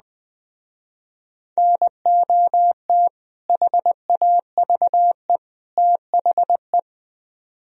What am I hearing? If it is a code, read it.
Morse code, heard as a keyed tone, about 21 wpm, 700 Hz. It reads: NOT HAVE THE
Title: Speed-Racing - Sets of 3 Words 17wpm